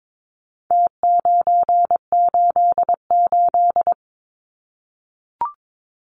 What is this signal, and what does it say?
Morse code, heard as a keyed tone, about 22 wpm, 700 Hz. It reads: T988
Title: Numbers - Contest Style - 4-Digit 22wpm